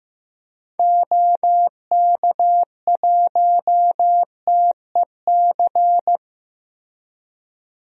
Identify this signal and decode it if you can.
Morse code, heard as a keyed tone, about 15 wpm, 700 Hz. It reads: OK1TEC
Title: DXCC-100 Call Signs - 6 Characters - Repeated 3x - 15wpm